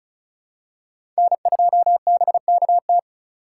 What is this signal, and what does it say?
Morse code, heard as a keyed tone, about 36 wpm, 700 Hz. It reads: N2BKT
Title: USA Call Signs - 5 Characters - 2x Character Spacing - 35wpm